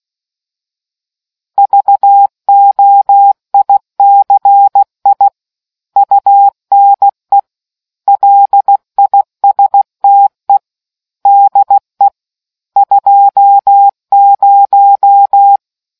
Morse code, heard as a keyed tone, about 16 wpm, 800 Hz. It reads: VOICI UNE LISTE DE 20